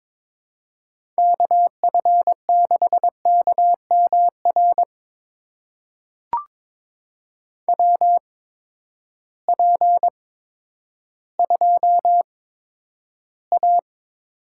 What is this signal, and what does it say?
Morse code, heard as a keyed tone, about 22 wpm, spaced as for 11 wpm, 700 Hz. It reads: KF6KMR WP2A
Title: USA Call Signs - 6 Characters - 8x Character Spacing - 22wpm